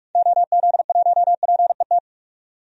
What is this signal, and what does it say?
Morse code, heard as a keyed tone, about 46 wpm, 700 Hz. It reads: OZ1PET
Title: DXCC-100 Call Signs - 6 Characters - 45wpm